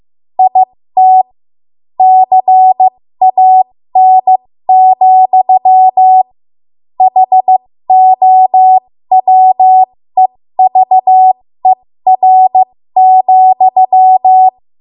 Morse code, heard as a keyed tone, about 15 wpm, spaced as for 13 wpm, 750 Hz. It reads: IT CAN, HOWEVER,
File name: Morse / 13WPM